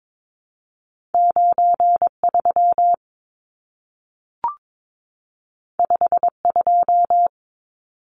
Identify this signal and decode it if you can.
Morse code, heard as a keyed tone, about 22 wpm, 700 Hz. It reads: 93 52
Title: Numbers - 2-Digit 22wpm